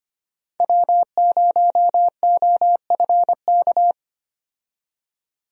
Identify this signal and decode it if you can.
Morse code, heard as a keyed tone, about 25 wpm, 700 Hz. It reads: W0OFK
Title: USA Call Signs - 5 Characters - 25wpm